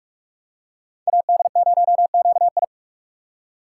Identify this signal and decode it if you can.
Morse code, heard as a keyed tone, about 47 wpm, 700 Hz. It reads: AD0QI
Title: USA Call Signs - 5 Characters - 3x Character Spacing - 45wpm